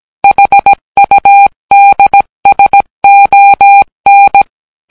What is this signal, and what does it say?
Morse code, heard as a keyed tone, about 17 wpm, 800 Hz. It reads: HUDSON